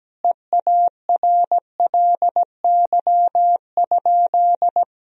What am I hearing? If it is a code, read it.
Morse code, heard as a keyed tone, about 17 wpm, 700 Hz. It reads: EARLY?
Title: Sentences from Top 1000 Words - Review Part 2 - 17wpm